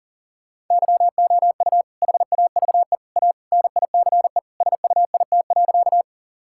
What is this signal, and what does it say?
Morse code, heard as a keyed tone, about 40 wpm, 700 Hz. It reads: YOU HAVE A NICE SUIT.
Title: Sentences from Top 1600 Words 40wpm